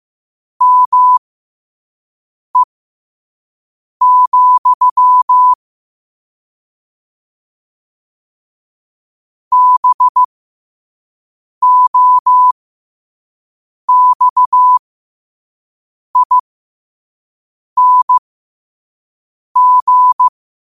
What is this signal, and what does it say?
Morse code, heard as a keyed tone, about 15 wpm, spaced as for 5 wpm, 1000 Hz. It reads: ME, BOXING
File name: qotdfiles / 5wpm